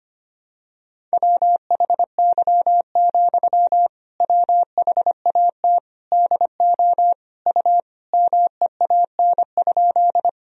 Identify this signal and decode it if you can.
Morse code, heard as a keyed tone, about 25 wpm, 700 Hz. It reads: WHY, WHAT DO U MEAN?
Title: Sentences from Top 200 Words and Abbreviations 25wpm